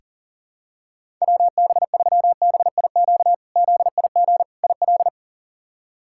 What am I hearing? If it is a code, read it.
Morse code, heard as a keyed tone, about 37 wpm, 700 Hz. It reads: WB3BIQ ZIG IL
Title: K1USN SST QSO Practice - 40wpm